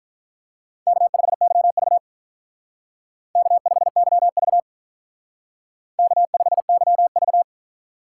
Morse code, heard as a keyed tone, about 46 wpm, 700 Hz. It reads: K5YV K5YV K5YV